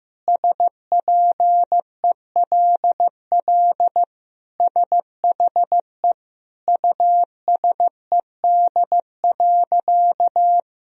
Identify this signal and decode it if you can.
Morse code, heard as a keyed tone, about 15 wpm, 700 Hz. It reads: SPELL SHE USED.
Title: Sentences from Top 1000 Words - Review Part 1 - 15wpm